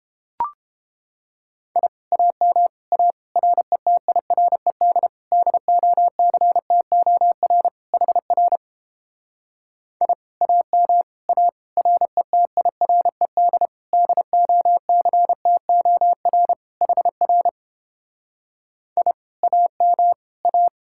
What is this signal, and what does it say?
Morse code, heard as a keyed tone, about 29 wpm, 700 Hz. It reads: I AM A RETIRED DOCTOR HR I AM A RETIRED DOCTOR HR I AM A